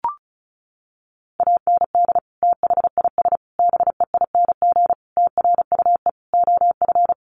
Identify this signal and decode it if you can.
Morse code, heard as a keyed tone, about 34 wpm, 700 Hz. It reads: AND THIS BEING TRUE OF
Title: Sentences from Top 1400 Words 35wpm